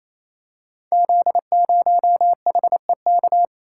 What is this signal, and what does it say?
Morse code, heard as a keyed tone, about 28 wpm, 700 Hz. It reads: Z0HEK
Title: Code Groups - 5 Letter:Numbers - 28wpm